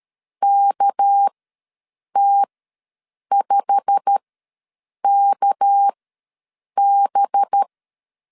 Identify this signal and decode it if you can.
Morse code, heard as a keyed tone, about 13 wpm, spaced as for 7 wpm, 800 Hz. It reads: KT5KB
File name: LEZIONI / 8 YJ45 / 8 YJ45 ws 12wpm 3146